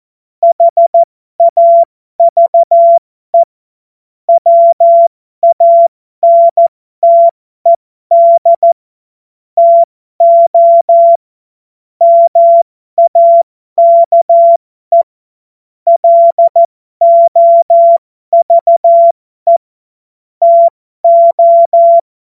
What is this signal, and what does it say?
Morse code, heard as a keyed tone, about 14 wpm, spaced as for 12 wpm, 680 Hz. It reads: HAVE WANTED TO MAKE LOVE TO
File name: Morsekurs / Witze